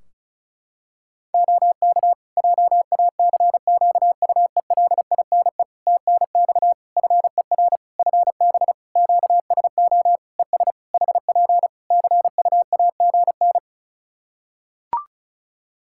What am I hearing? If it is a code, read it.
Morse code, heard as a keyed tone, about 36 wpm, 700 Hz. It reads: OK JACQUELINE TNX FER FB QSO ES HP CUAGN
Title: K1USN SST QSO Practice 35wpm